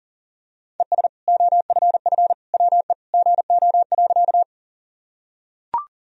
Morse code, heard as a keyed tone, about 40 wpm, 700 Hz. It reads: ES OFF WE GO.
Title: Sentences from Top 200 Words and Abbreviations 40wpm